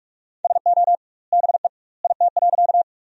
Morse code, heard as a keyed tone, about 43 wpm, 700 Hz. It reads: SO BE IT.
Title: 2x Word-Spacing - Sentences from Top 100 Words - 45wpm